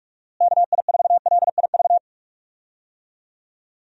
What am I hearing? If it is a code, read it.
Morse code, heard as a keyed tone, about 41 wpm, 700 Hz. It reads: KI4LIV
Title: USA Call Signs - 6 Characters - Repeated 3x - 45wpm